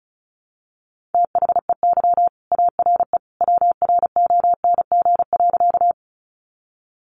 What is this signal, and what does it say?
Morse code, heard as a keyed tone, about 34 wpm, 700 Hz. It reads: THEY ARE WRONG.
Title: Sentences from Top 900 Words 35wpm